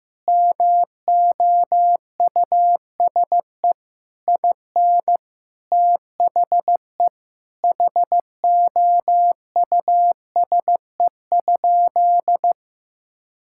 Morse code, heard as a keyed tone, about 15 wpm, 700 Hz. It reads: MOUSE IN THE HOUSE?
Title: Sentences from Top 1000 Words - Review Part 5 - 15wpm